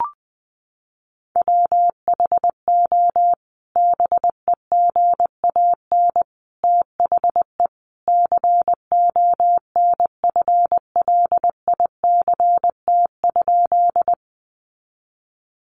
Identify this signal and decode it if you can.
Morse code, heard as a keyed tone, about 20 wpm, 700 Hz. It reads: WHO BEGAN THE CONFLICT?
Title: Sentences from Top 1300 Words 20wpm